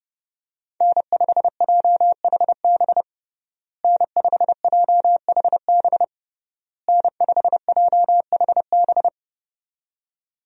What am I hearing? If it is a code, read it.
Morse code, heard as a keyed tone, about 32 wpm, 700 Hz. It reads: N5JHB N5JHB N5JHB
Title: USA Call Signs - 5 Characters - Repeated 3x - 30wpm